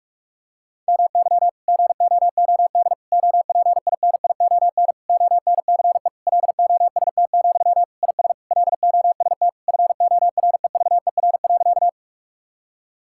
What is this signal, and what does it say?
Morse code, heard as a keyed tone, about 41 wpm, 700 Hz. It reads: MY GOOD OPINION ONCE LOST, IS LOST FOREVER.
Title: Sentences from Top 1600 Words 45wpm